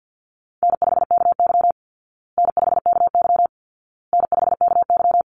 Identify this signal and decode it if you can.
Morse code, heard as a keyed tone, about 50 wpm, 700 Hz. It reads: N5KY N5KY N5KY
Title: Easy Call Signs Repeated 3x - 50wpm